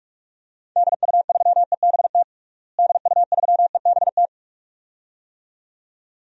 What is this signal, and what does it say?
Morse code, heard as a keyed tone, about 46 wpm, 700 Hz. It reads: DU3EBT DU3EBT
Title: DXCC-100 Call Signs - 6 Characters - Repeated 2x - 45wpm